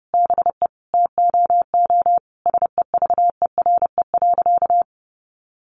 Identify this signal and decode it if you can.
Morse code, heard as a keyed tone, about 30 wpm, 700 Hz. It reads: BE TOO SEVERE.
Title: Sentences from Top 1400 Words 30wpm